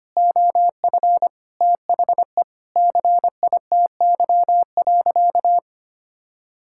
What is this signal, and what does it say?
Morse code, heard as a keyed tone, about 25 wpm, 700 Hz. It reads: OF THE CITY.
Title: Sentences from Top 1000 Words - Review Part 6 - 25wpm